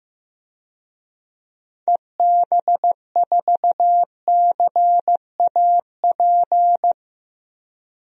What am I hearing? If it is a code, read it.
Morse code, heard as a keyed tone, about 15 wpm, 700 Hz. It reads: EB4CAP
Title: DXCC-100 Call Signs - 6 Characters - 15wpm